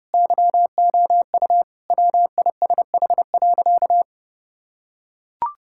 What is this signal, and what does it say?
Morse code, heard as a keyed tone, about 32 wpm, 700 Hz. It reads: YOU WISH.
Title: Sentences from Top 1600 Words 30wpm